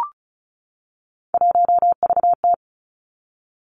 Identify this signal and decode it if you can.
Morse code, heard as a keyed tone, about 36 wpm, 700 Hz. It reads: 1VT
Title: Three Challenging Letter-Number Combinations 35wpm